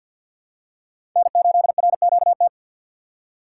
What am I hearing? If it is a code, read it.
Morse code, heard as a keyed tone, about 52 wpm, 700 Hz. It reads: N8RQT
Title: USA Call Signs - 5 Characters - 2x Character Spacing - 50wpm